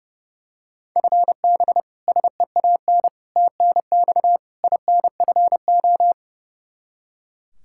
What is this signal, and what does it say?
Morse code, heard as a keyed tone, about 30 wpm, 700 Hz. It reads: FB SEAN TNX INFO
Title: Practice QSO - Exchange 3 - More Info - 30wpm